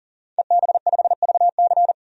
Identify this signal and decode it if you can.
Morse code, heard as a keyed tone, about 40 wpm, 700 Hz. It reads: EB5VC
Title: DXCC-100 Call Signs - 5 Characters - 40wpm